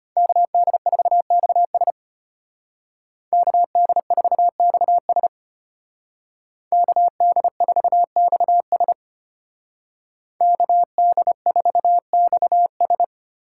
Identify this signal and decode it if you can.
Morse code, heard as a keyed tone, about 33 wpm, 700 Hz. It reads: KD4XS KD4XS KD4XS KD4XS